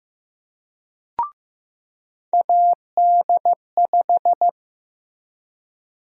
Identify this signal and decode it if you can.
Morse code, heard as a keyed tone, about 15 wpm, 700 Hz. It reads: AD5
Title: USA Call Sign Prefixes - Repeated 2x - 15wpm